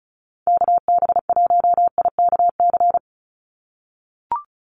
Morse code, heard as a keyed tone, about 34 wpm, 700 Hz. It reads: KB1IKC
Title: USA Call Signs 35wpm